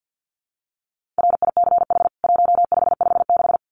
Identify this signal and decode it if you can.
Morse code, heard as a keyed tone, about 50 wpm, 700 Hz. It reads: RICH 1556
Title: CWT Exchange 50wpm